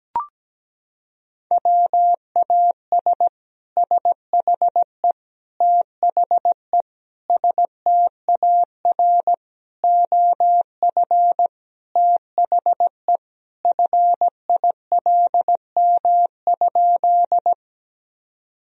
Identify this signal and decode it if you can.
Morse code, heard as a keyed tone, about 17 wpm, 700 Hz. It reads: WAS SHE THE STAR OF THE FILM?
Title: Sentences from Top 1000 Words - Review Part 3 - 17wpm